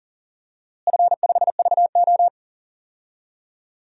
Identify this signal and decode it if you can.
Morse code, heard as a keyed tone, about 40 wpm, 700 Hz. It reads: F5VO
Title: DXCC-100 Call Signs - 4 Characters - Repeated 3x - 40wpm